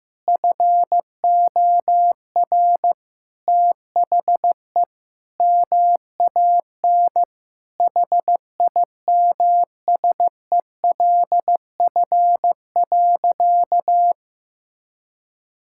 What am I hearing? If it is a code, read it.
Morse code, heard as a keyed tone, about 15 wpm, 700 Hz. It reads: FOR THE MAN HIMSELF.
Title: Sentences from Top 1600 Words 15wpm